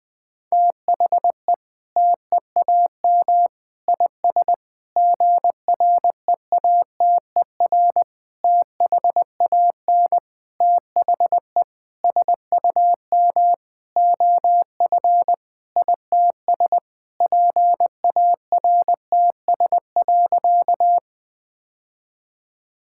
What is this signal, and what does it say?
Morse code, heard as a keyed tone, about 20 wpm, 700 Hz. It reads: THE TEAM IS GREATER THAN THE SUM OF ITS PARTS.
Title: Sentences with 8-Word Phrases - Spoken Beginning - 20wpm